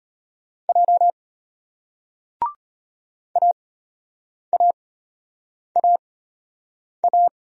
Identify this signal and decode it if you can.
Morse code, heard as a keyed tone, about 34 wpm, spaced as for 8 wpm, 700 Hz. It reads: J AAAA